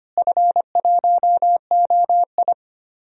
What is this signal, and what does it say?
Morse code, heard as a keyed tone, about 25 wpm, 700 Hz. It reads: F1OI
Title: DXCC-100 Call Signs - 4 Characters - Repeated 3x - 25wpm